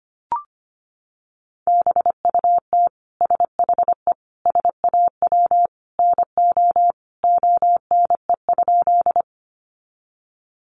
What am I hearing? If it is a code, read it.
Morse code, heard as a keyed tone, about 25 wpm, 700 Hz. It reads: BUT SHE SAW NO ONE?
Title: Sentences from Top 300 Words 25wpm